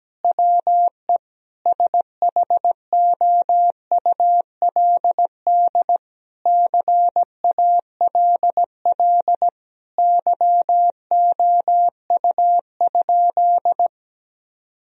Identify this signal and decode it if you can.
Morse code, heard as a keyed tone, about 17 wpm, 700 Hz. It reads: WE SHOULD CALL YOU?
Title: Sentences from Top 1000 Words - Review Part 3 - 17wpm